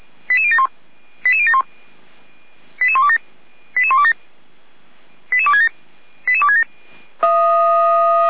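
Five-tone selective call (ZVEI), heard as quick runs of five tones, tones of about 70 ms. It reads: 88071 88071 88127 88127 88267 88267
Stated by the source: ZVEI five-tone selective call received on an SDR, from Feuerwehr Orlamünde (f85360).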